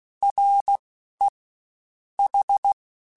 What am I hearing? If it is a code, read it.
Morse code, heard as a keyed tone, about 16 wpm, 800 Hz. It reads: R E H
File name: General / General-mp3s / Houses @15